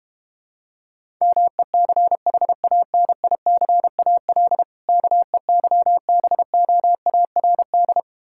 Morse code, heard as a keyed tone, about 32 wpm, 690 Hz. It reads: MECHANICAL KEYBOARD